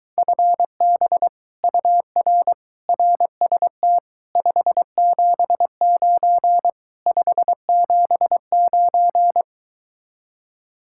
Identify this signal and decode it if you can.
Morse code, heard as a keyed tone, about 23 wpm, 700 Hz. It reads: FB UR RST 579 579